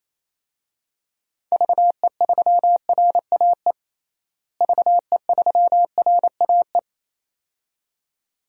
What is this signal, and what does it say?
Morse code, heard as a keyed tone, about 27 wpm, 700 Hz. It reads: VE3RAE VE3RAE
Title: DXCC-100 Call Signs - 6 Characters - Repeated 2x - 28wpm